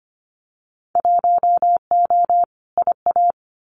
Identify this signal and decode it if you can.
Morse code, heard as a keyed tone, about 25 wpm, 690 Hz. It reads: 1O IA
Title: Winter Field Day Exchanges Encore 25wpm